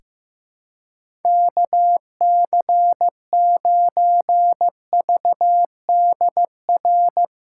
Morse code, heard as a keyed tone, about 15 wpm, 700 Hz. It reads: KC9VDR
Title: USA Callsign DE Callsign 15wpm